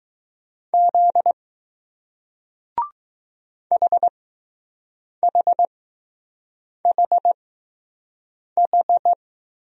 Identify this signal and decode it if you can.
Morse code, heard as a keyed tone, about 20 wpm, spaced as for 6 wpm, 700 Hz. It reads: Z HHHH